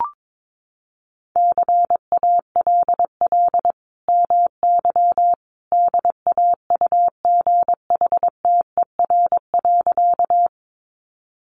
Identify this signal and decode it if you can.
Morse code, heard as a keyed tone, about 22 wpm, 700 Hz. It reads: CALL MY DAUGHTER.
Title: Sentences from Top 1500 Words 22wpm